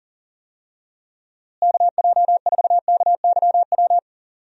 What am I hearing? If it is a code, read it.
Morse code, heard as a keyed tone, about 39 wpm, 700 Hz. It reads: KJ4KYW
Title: USA Call Signs - 6 Characters - Repeated 3x - 40wpm